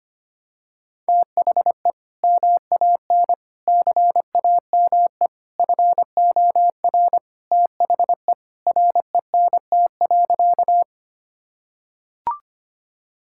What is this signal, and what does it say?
Morse code, heard as a keyed tone, about 25 wpm, 700 Hz. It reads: THE MAN CAME FOR THE RENT.
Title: Sentences from Top 1000 Words - Review Part 5 - 25wpm